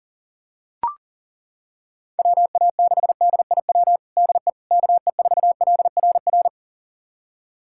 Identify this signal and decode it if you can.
Morse code, heard as a keyed tone, about 40 wpm, 700 Hz. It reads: WA6DIW DE KE4LRR
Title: USA Callsign DE Callsign 40wpm